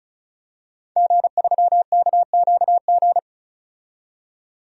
Morse code, heard as a keyed tone, about 36 wpm, 700 Hz. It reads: G3KQG
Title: DXCC-100 Call Signs - 5 Characters - 35wpm